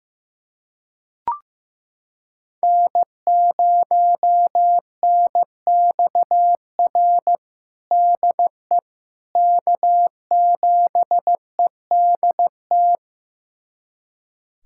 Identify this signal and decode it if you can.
Morse code, heard as a keyed tone, about 15 wpm, 700 Hz. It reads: N0NXR DE K7EDT
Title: USA Callsign DE Callsign - 5 Characters - 15wpm